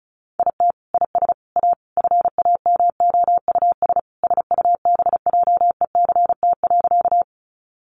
Morse code, heard as a keyed tone, about 34 wpm, 700 Hz. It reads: IT IS A FAMOUS SUBJECT.
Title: Sentences from Top 1500 Words 35wpm